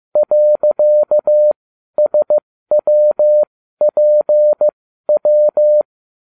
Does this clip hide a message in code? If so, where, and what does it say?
Yes: Morse code from 0.2 to 5.8 s, heard as a keyed tone, about 15 wpm, spaced as for 12 wpm, 600 Hz. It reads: .SWPW